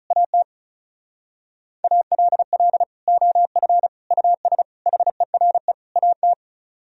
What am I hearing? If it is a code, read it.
Morse code, heard as a keyed tone, about 44 wpm, 700 Hz. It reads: AT ALL OF US HERE AT